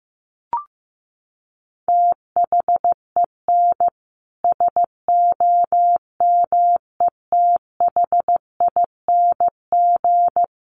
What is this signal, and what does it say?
Morse code, heard as a keyed tone, about 15 wpm, 700 Hz. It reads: THEN SOMETHING
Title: Sentences from Top 1100 Words 15wpm